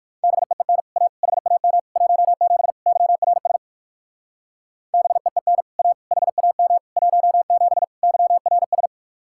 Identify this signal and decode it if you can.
Morse code, heard as a keyed tone, about 56 wpm, 700 Hz. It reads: BEEN A HAM 17 YRS BEEN A HAM 17 YRS
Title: Speed-Racing Common QSO Phrases - 45wpm